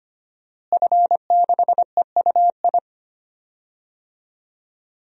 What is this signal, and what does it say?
Morse code, heard as a keyed tone, about 25 wpm, 700 Hz. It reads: F6EUI
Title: DXCC-100 Call Signs - 5 Characters - Repeated 2x - 25wpm